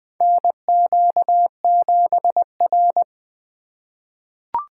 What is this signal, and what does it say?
Morse code, heard as a keyed tone, about 20 wpm, 700 Hz. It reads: NQ7R
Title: 2023 SOTA Chaser Call Signs - 20wpm